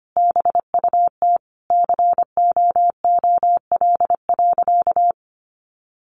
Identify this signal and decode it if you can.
Morse code, heard as a keyed tone, about 25 wpm, 700 Hz. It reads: BUT COOL.
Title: Sentences from Top 1200 Words 25wpm